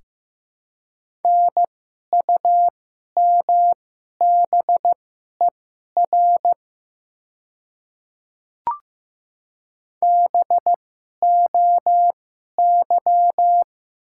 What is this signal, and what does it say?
Morse code, heard as a keyed tone, about 15 wpm, spaced as for 11 wpm, 700 Hz. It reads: NUMBER BOY
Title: 2x Character Spacing - Top 200 Words - 15wpm